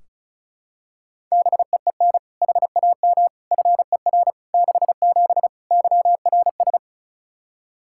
Morse code, heard as a keyed tone, about 34 wpm, 700 Hz. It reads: BEEN HAM FER 67 YRS
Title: Practice QSO - Exchange 3 - More Info - 35wpm